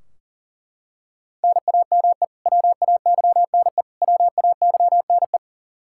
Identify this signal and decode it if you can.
Morse code, heard as a keyed tone, about 40 wpm, 700 Hz. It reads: NAME WAYNE WAYNE
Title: Practice QSO - Exchange 1 - Beginning - 40wpm